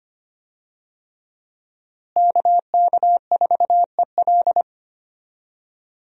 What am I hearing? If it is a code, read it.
Morse code, heard as a keyed tone, about 25 wpm, 700 Hz. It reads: KK4EL
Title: USA Call Signs - 5 Characters - Repeated 3x - 25wpm